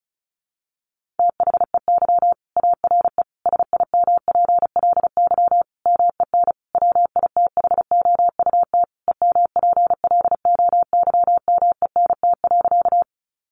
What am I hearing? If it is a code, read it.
Morse code, heard as a keyed tone, about 34 wpm, 700 Hz. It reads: THEY ARE SIMPLY MEN WITHOUT EMPLOYMENT.
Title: Sentences from Top 1500 Words 35wpm